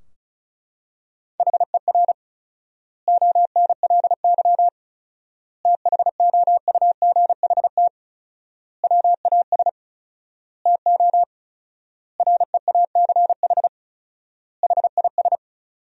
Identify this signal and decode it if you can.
Morse code, heard as a keyed tone, about 34 wpm, 700 Hz. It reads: HER ONLY THOUGHT WAS TO REACH HIS
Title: 4x Word-Spacing - Sentences from Top 400 Words - 35wpm